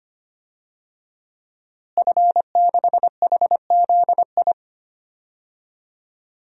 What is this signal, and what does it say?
Morse code, heard as a keyed tone, about 25 wpm, 700 Hz. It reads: F6HZI
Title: DXCC-100 Call Signs - 5 Characters - Repeated 3x - 25wpm